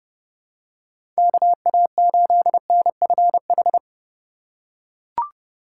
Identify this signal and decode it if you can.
Morse code, heard as a keyed tone, about 30 wpm, 700 Hz. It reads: KA8NFH
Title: USA Call Signs - 6 Characters - 2x Character Spacing - 30wpm